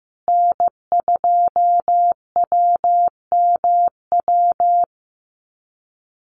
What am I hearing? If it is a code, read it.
Morse code, heard as a keyed tone, about 15 wpm, 700 Hz. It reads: N2WMW
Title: USA Call Signs Repeated 2x 15wpm